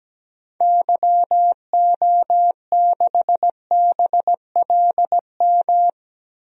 Morse code, heard as a keyed tone, about 17 wpm, 700 Hz. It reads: YO6BLM
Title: DXCC-100 Call Signs - 6 Characters - 17wpm